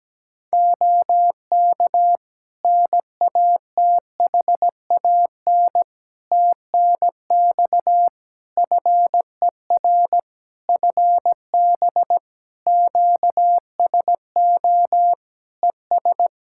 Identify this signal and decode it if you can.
Morse code, heard as a keyed tone, about 17 wpm, 700 Hz. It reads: OK NATHAN TNX FER FB QSO ES
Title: Practice QSO - Exchange 4 - Ending - 17wpm